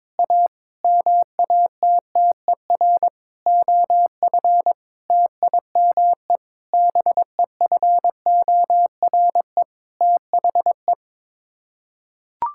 Morse code, heard as a keyed tone, about 22 wpm, 700 Hz. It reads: A MATTER OF TIME BEFORE THE
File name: Sets of 9 Words - 22wpm